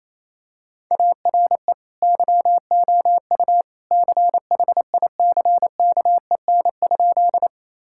Morse code, heard as a keyed tone, about 27 wpm, 700 Hz. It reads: ARE YOU CHICKEN?
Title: Sentences from Top 1000 Words - Review Part 5 - 28wpm